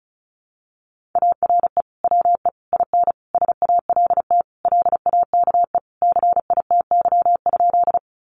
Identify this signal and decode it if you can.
Morse code, heard as a keyed tone, about 34 wpm, 700 Hz. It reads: ARE WE IN SALT LAKE CITY?
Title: Sentences from Top 1100 Words 35wpm